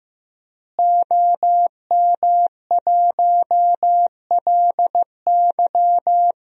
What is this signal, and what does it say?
Morse code, heard as a keyed tone, about 15 wpm, 700 Hz. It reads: OM1LY
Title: DXCC-100 Call Signs - 5 Characters - Repeated 3x - 15wpm